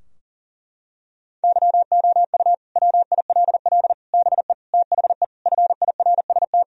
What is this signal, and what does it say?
Morse code, heard as a keyed tone, about 40 wpm, 700 Hz. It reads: YOU WILL BE THE FIRST
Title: Sentences from Top 1000 Words - Review Part 4 - 40wpm